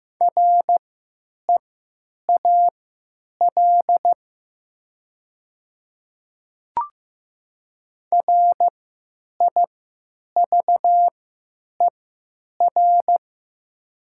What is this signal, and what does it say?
Morse code, heard as a keyed tone, about 15 wpm, spaced as for 9 wpm, 700 Hz. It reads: REAL RIVER
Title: 3x Character Spacing - Top 500 Words - 15wpm